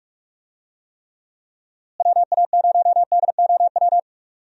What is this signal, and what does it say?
Morse code, heard as a keyed tone, about 46 wpm, 700 Hz. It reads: WA0DOW
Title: USA Call Signs - 6 Characters - Repeated 2x - 45wpm